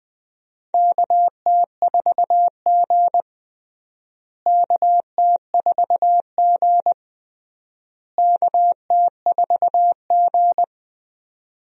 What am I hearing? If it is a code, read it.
Morse code, heard as a keyed tone, about 21 wpm, 700 Hz. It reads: KT4G KT4G KT4G
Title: USA Call Signs - 4 Characters - Repeated 3x - 20wpm